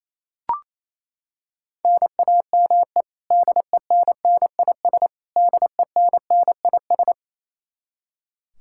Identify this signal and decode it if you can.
Morse code, heard as a keyed tone, about 28 wpm, 700 Hz. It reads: NAME DENNIS DENNIS
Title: Practice QSO - Exchange 1 - Beginning - 28wpm